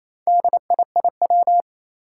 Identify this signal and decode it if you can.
Morse code, heard as a keyed tone, about 27 wpm, 700 Hz. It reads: DIIW
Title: Four Challenging Letter-Number Combinations 28wpm